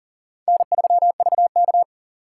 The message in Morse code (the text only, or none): N3VK